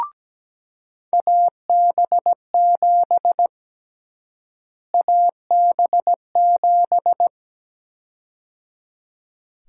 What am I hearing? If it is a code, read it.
Morse code, heard as a keyed tone, about 17 wpm, 700 Hz. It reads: AB7 AB7
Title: USA Call Sign Prefixes - Repeated 2x - 17wpm